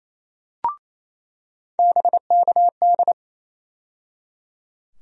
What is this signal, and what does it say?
Morse code, heard as a keyed tone, about 28 wpm, 700 Hz. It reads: BKD
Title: USA Call Sign Postfixes - 3 Letters 28wpm